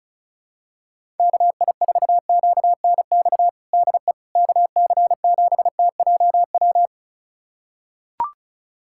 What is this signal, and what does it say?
Morse code, heard as a keyed tone, about 36 wpm, 700 Hz. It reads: KI4QNX DE KC7TJW
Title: USA Callsign DE Callsign - 6 Characters - 35wpm